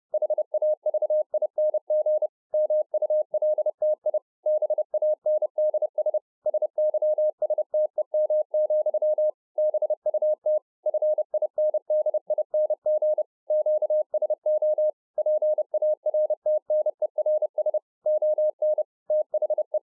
Morse code, heard as a keyed tone, about 30 wpm, 610 Hz. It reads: HAVING MULTI BANDS SYSTEM, BUT FINDING QSO PARTNERS ON THE